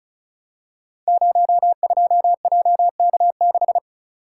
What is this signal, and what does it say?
Morse code, heard as a keyed tone, about 36 wpm, 690 Hz. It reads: 02JK6
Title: Code Groups - 5 Letter:Numbers - 35wpm